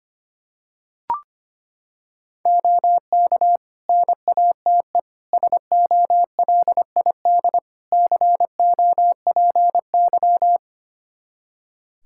Morse code, heard as a keyed tone, about 25 wpm, 700 Hz. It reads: OK NATE SOLID COPY
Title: Practice QSO - Exchange 3 - More Info - 25wpm